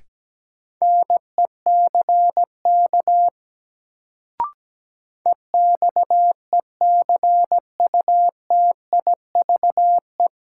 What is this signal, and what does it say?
Morse code, heard as a keyed tone, about 17 wpm, 700 Hz. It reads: NECK EXECUTIVE
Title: Sets of 5 Words Encore - New Word Drill - 17wpm